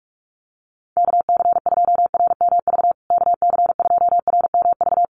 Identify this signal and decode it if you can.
Morse code, heard as a keyed tone, about 44 wpm, 700 Hz. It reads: KC2RMV KC2RMV
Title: USA Call Signs Repeated 3x 45wpm